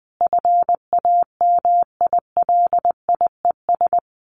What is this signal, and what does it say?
Morse code, heard as a keyed tone, about 20 wpm, 700 Hz. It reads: FAMILIES
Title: A Sound Of Thunder 20wpm